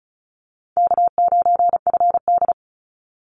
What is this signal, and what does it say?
Morse code, heard as a keyed tone, about 34 wpm, 700 Hz. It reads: K9FD
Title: Call Signs Repeated 3x - 35wpm